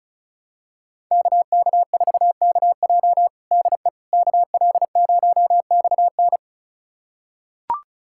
Morse code, heard as a keyed tone, about 36 wpm, 700 Hz. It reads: KK4KJ DE KL0XN
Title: USA Callsign DE Callsign - 5 Characters - 35wpm